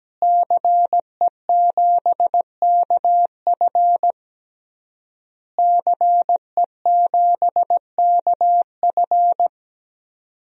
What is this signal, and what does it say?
Morse code, heard as a keyed tone, about 17 wpm, 700 Hz. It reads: CE7KF CE7KF
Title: DXCC-100 Call Signs - 5 Characters - Repeated 3x - 17wpm